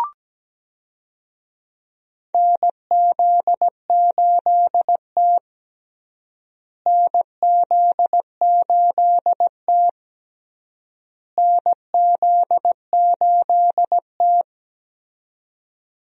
Morse code, heard as a keyed tone, about 17 wpm, 700 Hz. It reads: NZ8T NZ8T NZ8T
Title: USA Call Signs - 4 Characters - Repeated 3x - 17wpm